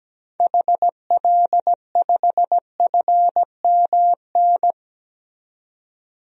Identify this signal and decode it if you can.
Morse code, heard as a keyed tone, about 17 wpm, 700 Hz. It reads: HL5FMN
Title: DXCC-100 Call Signs - 6 Characters - 17wpm